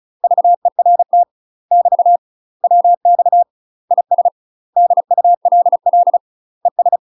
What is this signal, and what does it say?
Morse code, heard as a keyed tone, about 34 wpm, 700 Hz. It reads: VERT <BT> WX IS DULL ES
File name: UK MASTER QSOs TRAINING / 35WPM / UK MASTER QSOs 0024